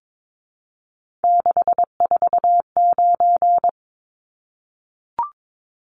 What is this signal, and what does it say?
Morse code, heard as a keyed tone, about 22 wpm, 700 Hz. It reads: 649
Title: Numbers - 3-Digit 22wpm